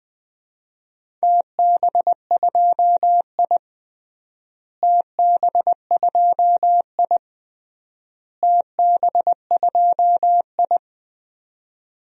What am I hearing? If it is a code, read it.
Morse code, heard as a keyed tone, about 20 wpm, 700 Hz. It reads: TB2I TB2I TB2I
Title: DXCC-100 Call Signs - 4 Characters - Repeated 3x - 20wpm